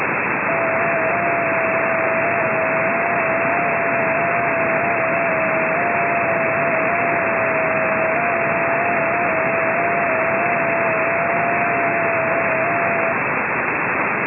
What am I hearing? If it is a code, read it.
FT8.
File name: ft8-vs-cw / 100-to-150 FT8 -5